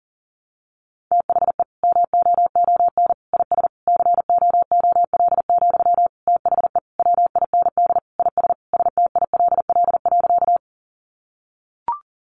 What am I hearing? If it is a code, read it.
Morse code, heard as a keyed tone, about 40 wpm, 700 Hz. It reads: THE MOON IS COOL, THE WIND IS STILL.
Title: Sentences from Top 1200 Words 40wpm